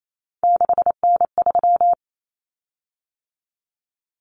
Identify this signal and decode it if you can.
Morse code, heard as a keyed tone, about 28 wpm, 700 Hz. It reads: 6N3
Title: Common Cut Numbers - 3-Digit 28wpm